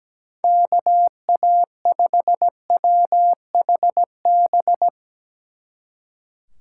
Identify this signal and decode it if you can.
Morse code, heard as a keyed tone, about 18 wpm, 700 Hz. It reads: KA5WHB
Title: Speed Racing - USA 2x3 Call Signs 17wpm